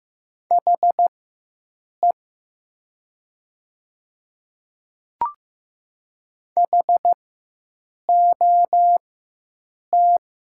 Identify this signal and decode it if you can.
Morse code, heard as a keyed tone, about 15 wpm, spaced as for 7 wpm, 700 Hz. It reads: HE HOT
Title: 4x Character Spacing - Top 100 Words - 15wpm